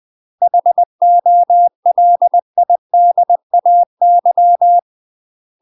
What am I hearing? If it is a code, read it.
Morse code, heard as a keyed tone, about 20 wpm, 700 Hz. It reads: HOLIDAY
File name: UK MASTER QSOs TRAINING / 20WPM / UK MASTER QSOs 0015